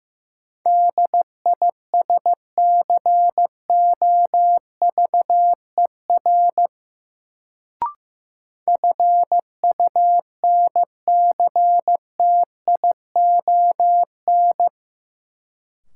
Morse code, heard as a keyed tone, about 15 wpm, 700 Hz. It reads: DISCOVER FUNCTION
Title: Sentences from Top 1100 Words - New Word Drill Alt - 15wpm